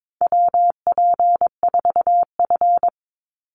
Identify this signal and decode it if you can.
Morse code, heard as a keyed tone, about 22 wpm, 700 Hz. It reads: WP4F
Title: Call Signs - 22wpm